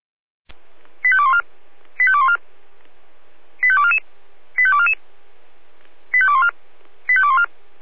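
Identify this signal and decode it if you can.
Five-tone selective call (ZVEI), heard as quick runs of five tones, tones of about 70 ms. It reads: 85214 85214 85244 85244 85214 85214
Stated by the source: ZVEI five-tone selective call received on an SDR.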